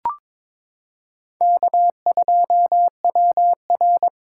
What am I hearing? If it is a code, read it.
Morse code, heard as a keyed tone, about 22 wpm, 700 Hz. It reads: K2WR
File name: practice / 22wpm / Four Letter-Number Combinations - 22wpm